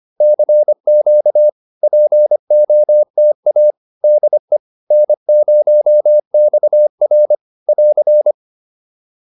Morse code, heard as a keyed tone, about 25 wpm, 600 Hz. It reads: CQ POTA DE N0XR <AR>